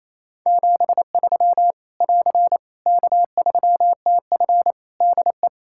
Morse code, heard as a keyed tone, about 28 wpm, 700 Hz. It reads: 73 <AR> K3TF DE
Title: Practice QSO - Exchange 4 - Ending - 28wpm